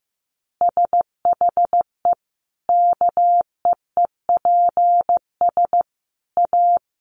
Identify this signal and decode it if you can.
Morse code, heard as a keyed tone, about 15 wpm, 700 Hz. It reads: SHE KEEPS A
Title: Easy Phrases 15wpm